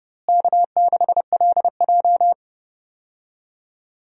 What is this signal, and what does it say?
Morse code, heard as a keyed tone, about 30 wpm, 700 Hz. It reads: K6LJ
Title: USA Call Signs - 4 Characters - Repeated 3x - 30wpm